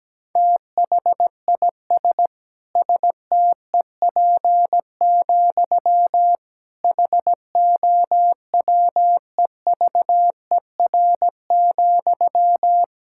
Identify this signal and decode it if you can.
Morse code, heard as a keyed tone, about 17 wpm, 700 Hz. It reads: THIS STEP, HOWEVER,
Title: Sentences from Top 1000 Words - Review Part 6 - 17wpm